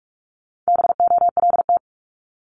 Morse code, heard as a keyed tone, about 46 wpm, 700 Hz. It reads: BOLT
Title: Multi-letter Phonemes 45wpm